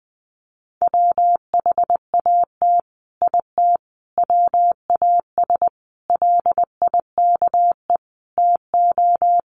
Morse code, heard as a keyed tone, about 19 wpm, 700 Hz. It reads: WHAT IT WAS LIKE TO
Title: Sets of 5 Words Encore 20wpm